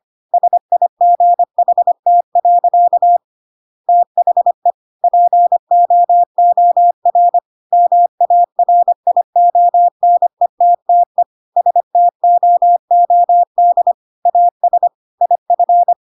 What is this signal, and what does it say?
Morse code, heard as a keyed tone, about 25 wpm, 700 Hz. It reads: SIGHT. THE POOR MARIONETTE STOOD AS IF
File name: Pinocchio / Pinocchio ch5 25 25 ik1hge